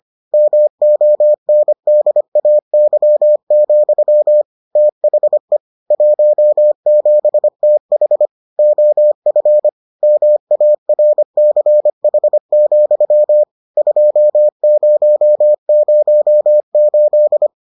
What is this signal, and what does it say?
Morse code, heard as a keyed tone, about 25 wpm, 600 Hz. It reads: MONDAY, THE 17TH OF MARCH, 2008